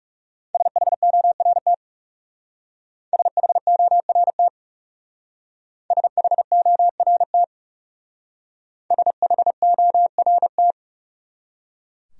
Morse code, heard as a keyed tone, about 35 wpm, 700 Hz. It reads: SHORT SHORT SHORT SHORT
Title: Speed-Racing Top 500 Words 30wpm